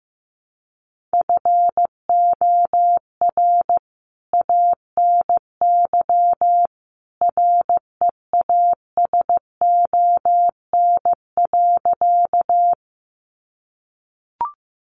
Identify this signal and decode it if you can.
Morse code, heard as a keyed tone, about 15 wpm, 700 Hz. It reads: FOR ANY REASON.
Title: A Sound Of Thunder 15wpm